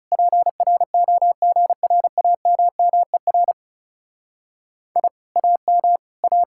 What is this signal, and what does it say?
Morse code, heard as a keyed tone, about 37 wpm, 700 Hz. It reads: PROGRAMMER I AM A
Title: Speed-Racing Common QSO Phrases Encore - 30wpm